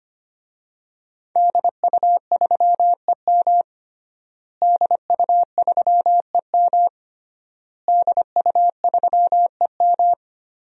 Morse code, heard as a keyed tone, about 25 wpm, 700 Hz. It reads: DU3EM DU3EM DU3EM
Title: DXCC-100 Call Signs - 5 Characters - Repeated 3x - 25wpm